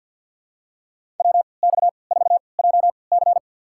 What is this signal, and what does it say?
Morse code, heard as a keyed tone, about 51 wpm, spaced as for 29 wpm, 690 Hz. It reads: WX4JC
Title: USA Call Signs - 5 Characters - 3x Character Spacing - 50wpm